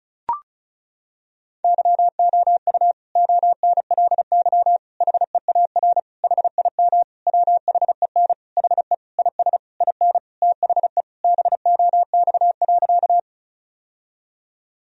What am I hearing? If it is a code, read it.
Morse code, heard as a keyed tone, about 36 wpm, 700 Hz. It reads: YOU ONLY HEAR HIM WHEN HE IS IN THE BOX.
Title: Sentences from Top 1000 Words - Review Part 2 - 35wpm